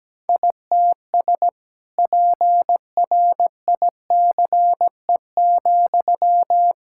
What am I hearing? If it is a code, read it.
Morse code, heard as a keyed tone, about 17 wpm, 700 Hz. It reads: ITS PRICE,
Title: Sentences from Top 1000 Words - Review Part 5 - 17wpm